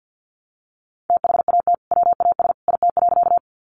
Morse code, heard as a keyed tone, about 50 wpm, 700 Hz. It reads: THAT WAS IT.
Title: Sentences from Top 100 Words 50wpm